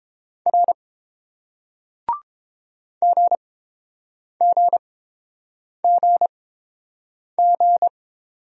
Morse code, heard as a keyed tone, about 30 wpm, spaced as for 7 wpm, 700 Hz. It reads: R GGGG